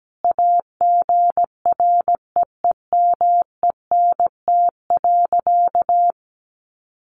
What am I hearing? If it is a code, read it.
Morse code, heard as a keyed tone, about 17 wpm, 700 Hz. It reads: AGREEMENT.
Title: Sentences from Top 1500 Words 17wpm